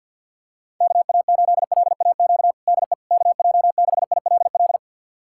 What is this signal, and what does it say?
Morse code, heard as a keyed tone, about 52 wpm, 700 Hz. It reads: KA8LAQ DE KJ6ILL
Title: USA Callsign DE Callsign 50wpm